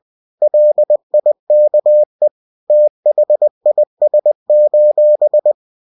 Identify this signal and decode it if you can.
Morse code, heard as a keyed tone, about 20 wpm, 600 Hz. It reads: LIKE THIS: